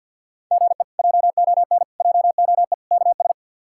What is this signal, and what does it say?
Morse code, heard as a keyed tone, about 47 wpm, 700 Hz. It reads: GE JON JOE KS
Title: K1USN SST QSO Practice - 50wpm